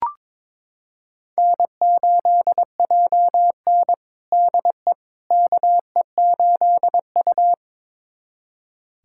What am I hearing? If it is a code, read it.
Morse code, heard as a keyed tone, about 22 wpm, 700 Hz. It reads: N8JN DE KE8U
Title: USA Callsign DE Callsign - 4 Characters - 22wpm